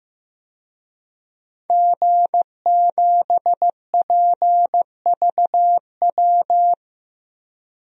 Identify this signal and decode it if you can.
Morse code, heard as a keyed tone, about 15 wpm, 700 Hz. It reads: G7PVW
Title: DXCC-100 Call Signs - 5 Characters - Repeated 3x - 15wpm